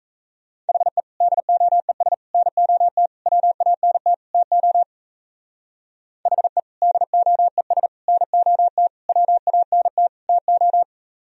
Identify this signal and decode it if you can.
Morse code, heard as a keyed tone, about 36 wpm, 700 Hz. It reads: HE DOES NOT WANT TO HE DOES NOT WANT TO